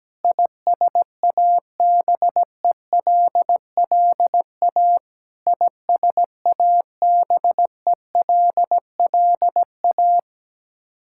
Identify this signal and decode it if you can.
Morse code, heard as a keyed tone, about 17 wpm, 700 Hz. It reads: ISABELLA ISABELLA
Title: Practice QSO - Exchange 1 - Beginning - 17wpm